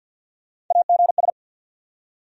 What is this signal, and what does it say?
Morse code, heard as a keyed tone, about 50 wpm, 700 Hz. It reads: AGS
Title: USA Call Sign Postfixes - 3 Letters 50wpm